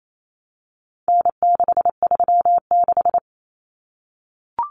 Morse code, heard as a keyed tone, about 28 wpm, 700 Hz. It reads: N636
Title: Common Cut Numbers - 4-Digit 28wpm